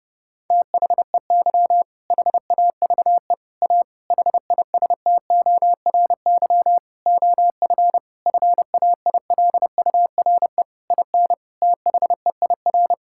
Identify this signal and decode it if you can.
Morse code, heard as a keyed tone, about 32 wpm, 700 Hz. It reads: THEY HAVE A HISTORY OF FAILURE IN THEIR
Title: Sentences from Top 1000 Words - Review Part 6 - 30wpm